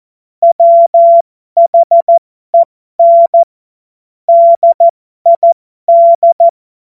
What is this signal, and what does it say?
Morse code, heard as a keyed tone, about 14 wpm, spaced as for 12 wpm, 680 Hz. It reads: WHEN DID
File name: Morsekurs / Witze